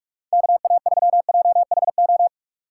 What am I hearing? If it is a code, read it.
Morse code, heard as a keyed tone, about 45 wpm, 700 Hz. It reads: KA3JHO
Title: USA Call Signs - 2x Character Spacing - 45wpm